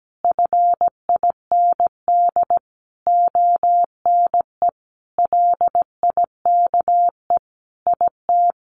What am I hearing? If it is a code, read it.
Morse code, heard as a keyed tone, about 17 wpm, 700 Hz. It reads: FIND ONE LIKE IT
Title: Sentences from Top 1400 Words 17wpm